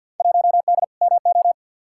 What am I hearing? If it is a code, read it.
Morse code, heard as a keyed tone, about 52 wpm, 700 Hz. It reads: 1D MO
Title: Field Day Exchanges - 50wpm